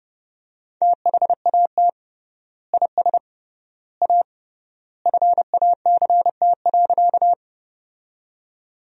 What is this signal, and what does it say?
Morse code, heard as a keyed tone, about 29 wpm, 700 Hz. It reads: THAT IS A FACT.
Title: 3x Word-Spacing - Sentences from Top 500 Words - 30wpm